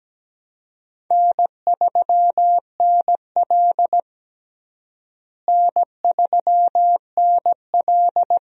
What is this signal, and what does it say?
Morse code, heard as a keyed tone, about 17 wpm, 700 Hz. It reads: N3NL N3NL
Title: USA Call Signs - 4 Characters - Repeated 2x - 17wpm